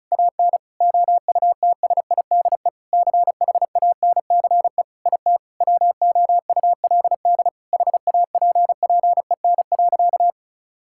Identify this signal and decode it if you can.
Morse code, heard as a keyed tone, about 36 wpm, 700 Hz. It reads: AN OUTSIDE CHANCE IT WOULD HAPPEN.
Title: Sentences from Top 1000 Words - Review Part 6 - 35wpm